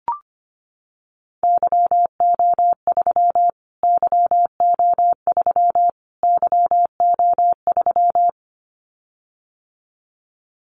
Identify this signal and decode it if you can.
Morse code, heard as a keyed tone, about 25 wpm, 700 Hz. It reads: YO3 YO3 YO3
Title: Callsign Prefixes for Top 25 DXCC - Repeated 3x - 25wpm